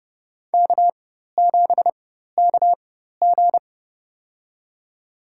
Morse code, heard as a keyed tone, about 31 wpm, spaced as for 14 wpm, 700 Hz. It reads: K7KG
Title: USA Call Signs - 4 Characters - 4x Character Spacing - 30wpm